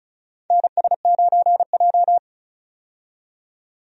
Morse code, heard as a keyed tone, about 36 wpm, 700 Hz. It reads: NS9J
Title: USA Call Signs - 4 Characters - 35wpm